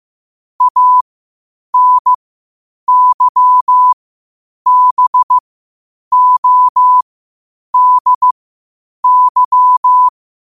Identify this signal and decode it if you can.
Morse code, heard as a keyed tone, about 15 wpm, spaced as for 8 wpm, 1000 Hz. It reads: ANYBODY
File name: qotdfiles / 8wpm